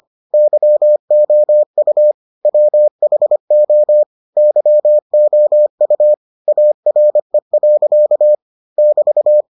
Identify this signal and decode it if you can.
Morse code, heard as a keyed tone, about 25 wpm, 600 Hz. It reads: YOU WHO YOU ARE. <BT>